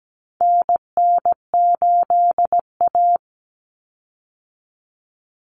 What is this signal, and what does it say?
Morse code, heard as a keyed tone, about 17 wpm, 700 Hz. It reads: NN8A
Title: Common Cut Numbers - 4-Digit 17wpm